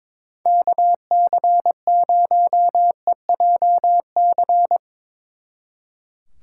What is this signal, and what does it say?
Morse code, heard as a keyed tone, about 24 wpm, 700 Hz. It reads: KC0EJC
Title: Speed Racing - USA 2x3 Call Signs 22wpm